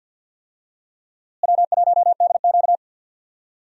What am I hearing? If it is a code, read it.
Morse code, heard as a keyed tone, about 47 wpm, 700 Hz. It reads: W1DQ
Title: USA Call Signs - 4 Characters - Repeated 2x - 50wpm